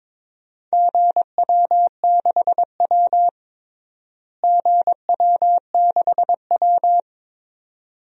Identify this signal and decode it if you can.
Morse code, heard as a keyed tone, about 22 wpm, 700 Hz. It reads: GW6W GW6W
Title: DXCC-100 Call Signs - 4 Characters - Repeated 2x - 22wpm